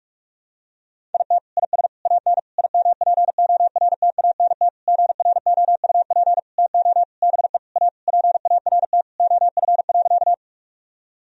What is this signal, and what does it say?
Morse code, heard as a keyed tone, about 41 wpm, 700 Hz. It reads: IT IS AN IMPORTANT GROUP TO BE A PART OF.
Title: Sentences from Top 1000 Words - Review Part 1 - 45wpm